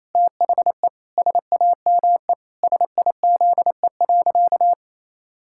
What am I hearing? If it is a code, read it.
Morse code, heard as a keyed tone, about 28 wpm, 700 Hz. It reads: THE SAME SIZE.
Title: Pangram Challenge v2 - 28wpm